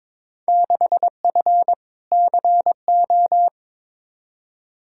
Morse code, heard as a keyed tone, about 21 wpm, 700 Hz. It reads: 6F CO
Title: Field Day Exchanges Encore - 22wpm